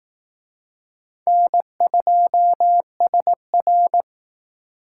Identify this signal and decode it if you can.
Morse code, heard as a keyed tone, about 19 wpm, 700 Hz. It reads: N2SR